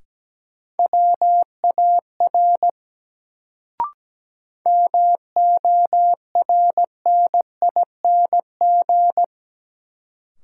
Morse code, heard as a keyed tone, about 17 wpm, 700 Hz. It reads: WAR MORNING
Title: Sentences from Top 400 Words - New Word Drill - 17wpm